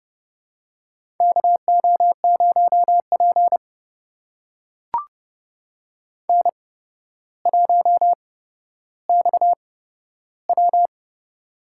Morse code, heard as a keyed tone, about 31 wpm, spaced as for 13 wpm, 700 Hz. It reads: KO0P N1XW
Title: USA Call Signs - 4 Characters - 8x Character Spacing - 30wpm